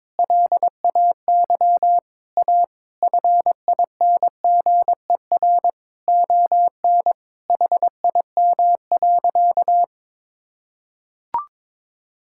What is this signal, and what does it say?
Morse code, heard as a keyed tone, about 22 wpm, 700 Hz. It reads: LAY A FINGER ON HIM.
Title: Sentences from Top 1600 Words 22wpm